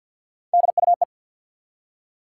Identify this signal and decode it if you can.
Morse code, heard as a keyed tone, about 47 wpm, 700 Hz. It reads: DUE
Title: Three Letter Words - 50wpm